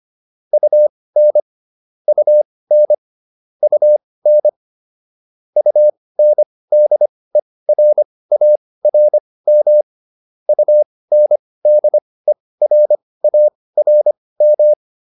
Morse code, heard as a keyed tone, about 25 wpm, spaced as for 18 wpm, 600 Hz. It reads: UN UN UN UNDERARM UNDERARM